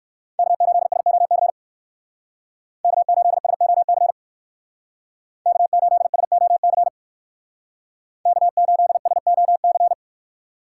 Morse code, heard as a keyed tone, about 58 wpm, 700 Hz. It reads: K8SOC K8SOC K8SOC K8SOC